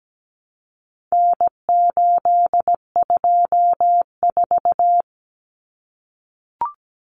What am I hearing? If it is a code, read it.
Morse code, heard as a keyed tone, about 17 wpm, 700 Hz. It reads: N824
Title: Common Cut Numbers - 4-Digit 17wpm